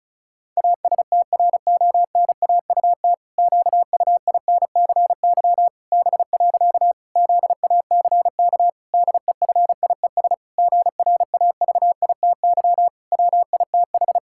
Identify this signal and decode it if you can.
Morse code, heard as a keyed tone, about 36 wpm, 700 Hz. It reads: ASTRONAUT QUINCY B. ZACK DEFIES GRAVITY WITH
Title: Pangram Challenge v2 - 35wpm